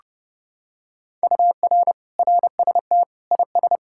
Morse code, heard as a keyed tone, about 28 wpm, 700 Hz. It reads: UR RST IS